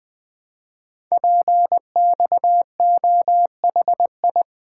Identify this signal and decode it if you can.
Morse code, heard as a keyed tone, about 20 wpm, 700 Hz. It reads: PXOHI
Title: Code Groups - 5 Letters - 20wpm